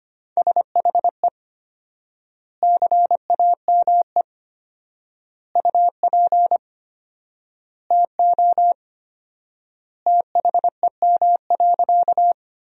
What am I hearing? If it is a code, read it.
Morse code, heard as a keyed tone, about 25 wpm, 700 Hz. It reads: SHE CAME UP TO THEM.
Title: 4x Word-Spacing - Sentences from Top 200 Words - 25wpm